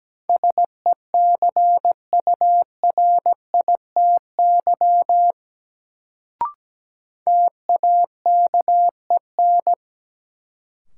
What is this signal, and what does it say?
Morse code, heard as a keyed tone, about 17 wpm, 700 Hz. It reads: SECURITY TAKEN
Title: Sentences from Top 1000 Words - Review Part 4 - New Word Drill - 17wpm